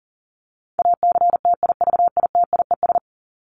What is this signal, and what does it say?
Morse code, heard as a keyed tone, about 40 wpm, 700 Hz. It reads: ACTIVITIES
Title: ES:S Suffix - Top 100 Words with Base Words 40wpm